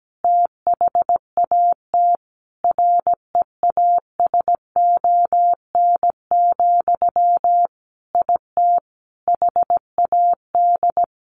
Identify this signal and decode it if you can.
Morse code, heard as a keyed tone, about 17 wpm, 700 Hz. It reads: THAT REASON, IT HAD
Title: Sentences from Top 1500 Words 17wpm